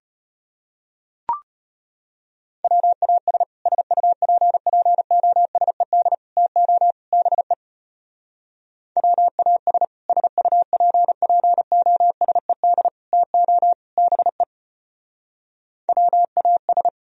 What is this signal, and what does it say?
Morse code, heard as a keyed tone, about 31 wpm, 700 Hz. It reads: WAS SUPPOSED TO BE WAS SUPPOSED TO BE WAS